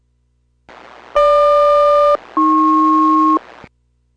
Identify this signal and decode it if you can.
Aircraft SELCAL, heard as two pairs of tones. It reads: GQ-AP